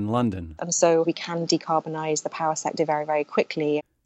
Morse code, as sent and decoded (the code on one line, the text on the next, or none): none